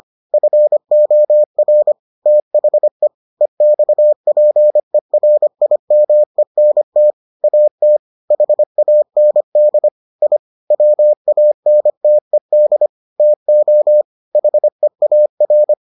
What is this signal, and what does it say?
Morse code, heard as a keyed tone, about 25 wpm, 600 Hz. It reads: FOR THE EXPERIMENT AT HAND I WANTED TO HEAR